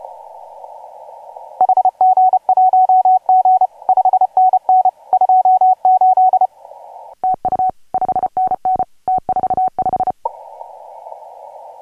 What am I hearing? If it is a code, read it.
Morse code, heard as a keyed tone, about 30 wpm, 750 Hz. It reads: HG1G 5NN 28 TU 5NN T45